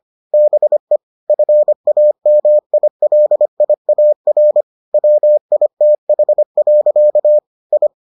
Morse code, heard as a keyed tone, about 25 wpm, 600 Hz. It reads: BE FAMILIAR WITH. I